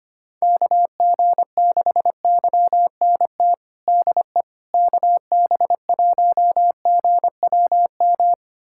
Morse code, heard as a keyed tone, about 25 wpm, 700 Hz. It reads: KG6YNT DE KB1GWM
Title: USA Callsign DE Callsign - 6 Characters - 25wpm